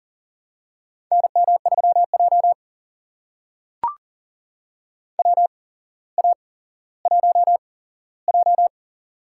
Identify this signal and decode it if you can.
Morse code, heard as a keyed tone, about 40 wpm, spaced as for 17 wpm, 700 Hz. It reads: NM3J WA1J
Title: USA Call Signs - 4 Characters - 8x Character Spacing - 40wpm